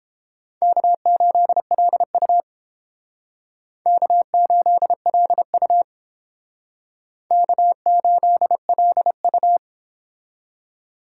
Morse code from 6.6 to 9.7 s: K8LU

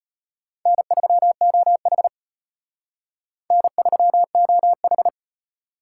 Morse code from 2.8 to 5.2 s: N3OH

